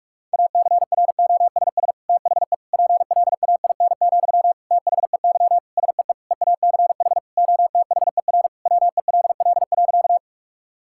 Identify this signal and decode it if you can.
Morse code, heard as a keyed tone, about 42 wpm, 700 Hz. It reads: ACROSS THE PLAIN, THEY SEE EACH OTHER WELL.
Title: Sentences from Top 1000 Words - Review Part 5 - 45wpm